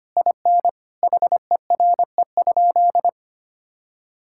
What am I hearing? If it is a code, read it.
Morse code, heard as a keyed tone, about 25 wpm, 700 Hz. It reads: IN HERE?
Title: Sentences from Top 1000 Words - Review Part 6 - 25wpm